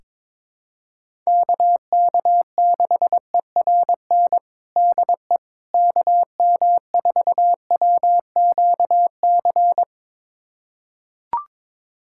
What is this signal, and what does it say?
Morse code, heard as a keyed tone, about 22 wpm, 700 Hz. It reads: KK6ERN DE KM4WQC
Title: USA Callsign DE Callsign - 6 Characters - 22wpm